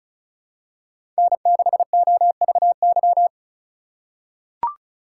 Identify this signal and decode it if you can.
Morse code, heard as a keyed tone, about 36 wpm, 700 Hz. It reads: N6OVY
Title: USA Call Signs - 5 Characters - 2x Character Spacing - 35wpm